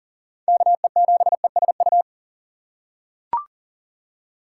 Morse code, heard as a keyed tone, about 40 wpm, 700 Hz. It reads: KE7ESU
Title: USA Call Signs - 6 Characters - 4x Character Spacing - 40wpm